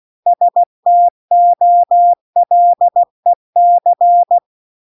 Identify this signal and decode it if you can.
Morse code, heard as a keyed tone, about 16 wpm, 700 Hz. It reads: STOLEC